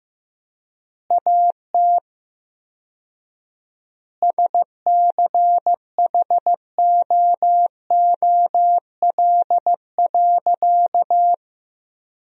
Morse code, heard as a keyed tone, about 15 wpm, 700 Hz. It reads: AT SCHOOL.
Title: 4x Word-Spacing - Sentences from Top 300 Words - 15wpm